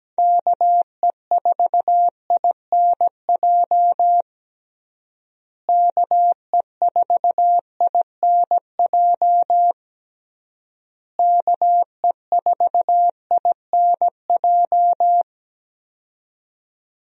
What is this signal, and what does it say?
Morse code, heard as a keyed tone, about 17 wpm, 700 Hz. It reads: KE4INJ KE4INJ KE4INJ
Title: USA Call Signs - 6 Characters - Repeated 3x - 17wpm